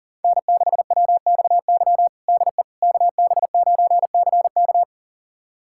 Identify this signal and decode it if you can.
Morse code, heard as a keyed tone, about 40 wpm, 700 Hz. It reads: N6WXY DE KB9CK
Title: USA Callsign DE Callsign - 5 Characters - 40wpm